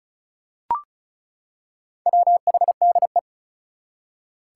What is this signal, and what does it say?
Morse code, heard as a keyed tone, about 34 wpm, 700 Hz. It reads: WHDE
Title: Four Challenging Letter-Number Combinations 35wpm